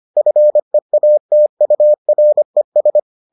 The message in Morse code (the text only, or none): FEATURES